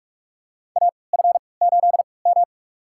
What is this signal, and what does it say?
Morse code, heard as a keyed tone, about 44 wpm, spaced as for 25 wpm, 700 Hz. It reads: AF8M
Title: USA Call Signs - 4 Characters - 3x Character Spacing - 45wpm